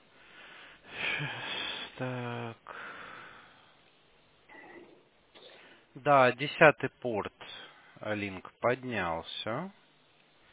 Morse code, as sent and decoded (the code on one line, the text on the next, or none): none